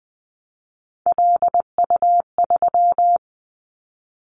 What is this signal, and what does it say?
Morse code, heard as a keyed tone, about 20 wpm, 700 Hz. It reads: LU3
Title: Callsign Prefixes for Top 50 DXCC - 20wpm